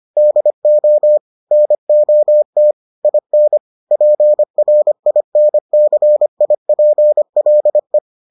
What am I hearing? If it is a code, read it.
Morse code, heard as a keyed tone, about 25 wpm, 600 Hz. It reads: DO NOT IN PRINCIPLE